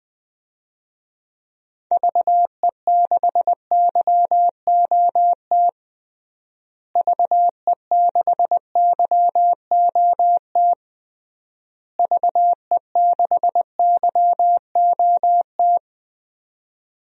Morse code, heard as a keyed tone, about 20 wpm, 700 Hz. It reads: VE6YOT VE6YOT VE6YOT
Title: DXCC-100 Call Signs - 6 Characters - Repeated 3x - 20wpm